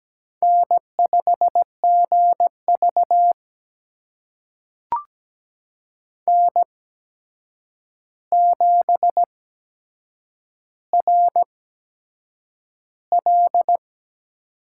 Morse code, heard as a keyed tone, about 17 wpm, spaced as for 7 wpm, 700 Hz. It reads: N5GV N7RL
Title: USA Call Signs - 4 Characters - 8x Character Spacing - 17wpm